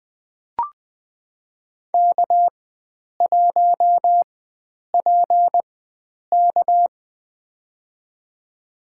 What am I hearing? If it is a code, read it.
Morse code, heard as a keyed tone, about 20 wpm, spaced as for 9 wpm, 700 Hz. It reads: K1PK
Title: USA Call Signs - 4 Characters - 4x Character Spacing - 20wpm